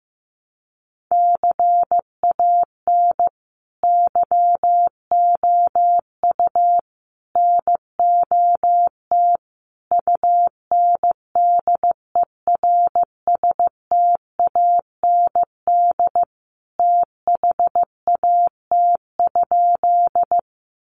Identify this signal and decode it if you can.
Morse code, heard as a keyed tone, about 15 wpm, 700 Hz. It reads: CAN YOU NOT UNDERSTAND THAT?
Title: Sentences from Top 600 Words 15wpm